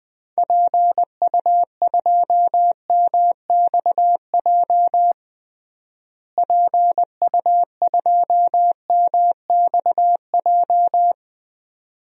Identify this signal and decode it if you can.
Morse code, heard as a keyed tone, about 20 wpm, 690 Hz. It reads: PU2MXJ PU2MXJ
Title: DXCC-100 Call Signs - 6 Characters - Repeated 2x - 20wpm